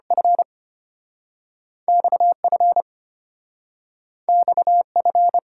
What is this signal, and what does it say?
Morse code, heard as a keyed tone, about 33 wpm, 700 Hz. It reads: F XF XF